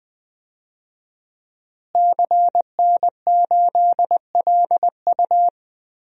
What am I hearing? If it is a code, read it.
Morse code, heard as a keyed tone, about 21 wpm, 700 Hz. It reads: CN8LU
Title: DXCC-100 Call Signs - 5 Characters - Repeated 2x - 20wpm